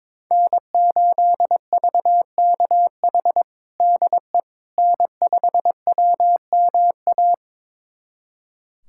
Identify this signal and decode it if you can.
Morse code, heard as a keyed tone, about 21 wpm, 700 Hz. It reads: N8VKH DE N5WMA
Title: USA Callsign DE Callsign - 5 Characters - 22wpm